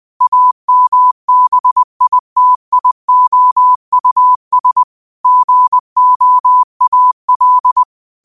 Morse code, heard as a keyed tone, about 20 wpm, 1000 Hz. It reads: AMBITIOUS GOAL
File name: qotdfiles / 20wpm